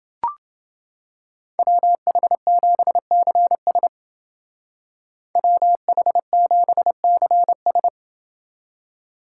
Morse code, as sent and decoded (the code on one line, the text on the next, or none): .-- .... --... -.-. ... / .-- .... --... -.-. ...
WH7CS WH7CS